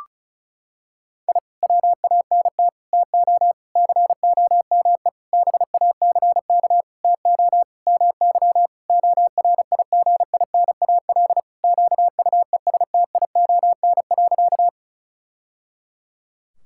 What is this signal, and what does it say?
Morse code, heard as a keyed tone, about 36 wpm, 700 Hz. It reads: I WANT TO COME BACK TO MY ORIGINAL QUESTION.
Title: Sentences from Top 1000 Words - Review Part 6 - 35wpm